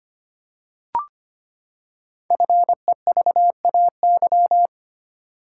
Morse code, heard as a keyed tone, about 25 wpm, 700 Hz. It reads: FEVAY
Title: Code Groups - 5 Letter:Numbers - 25wpm